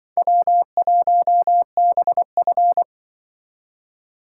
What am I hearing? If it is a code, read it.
Morse code, heard as a keyed tone, about 25 wpm, 700 Hz. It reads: W1BF
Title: Speed Racing - USA 1x2 Call Signs 20wpm